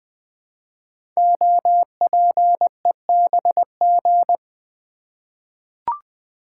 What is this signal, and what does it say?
Morse code, heard as a keyed tone, about 20 wpm, 700 Hz. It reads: OPEBG
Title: Code Groups - 5 Letter:Numbers - 20wpm